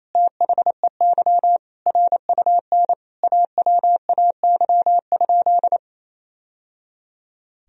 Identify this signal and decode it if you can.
Morse code, heard as a keyed tone, about 28 wpm, 700 Hz. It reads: THEY RUN AWAY?
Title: Sentences from Top 1000 Words - Review Part 1 - 28wpm